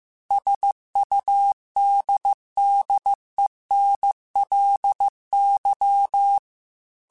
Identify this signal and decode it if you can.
Morse code, heard as a keyed tone, about 15 wpm, 800 Hz. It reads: SUDDENLY
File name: Technical Subjects / Technical Subjects-mp3s / AND Gate @13